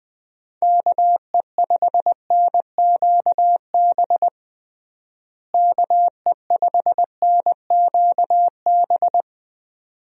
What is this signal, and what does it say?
Morse code, heard as a keyed tone, about 20 wpm, 700 Hz. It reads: KE5NQB KE5NQB
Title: USA Call Signs - 6 Characters - Repeated 2x - 20wpm